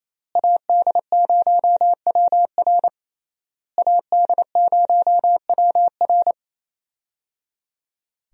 Morse code, heard as a keyed tone, about 28 wpm, 700 Hz. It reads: AD0WR AD0WR
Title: USA Call Signs - 5 Characters - Repeated 3x - 28wpm